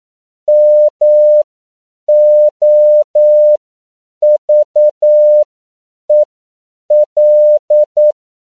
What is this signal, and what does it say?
Morse code, heard as a keyed tone, about 9 wpm, spaced as for 7 wpm, 600 Hz. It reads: MOVEL
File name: CW / 10PPM